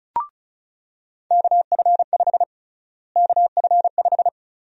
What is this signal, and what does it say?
Morse code, heard as a keyed tone, about 34 wpm, 700 Hz. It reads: KF5 KF5
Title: USA Call Sign Prefixes - Repeated 2x - 35wpm